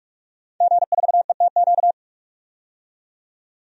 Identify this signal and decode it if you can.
Morse code, heard as a keyed tone, about 46 wpm, 700 Hz. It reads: G4ETQ
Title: DXCC-100 Call Signs - 5 Characters - 45wpm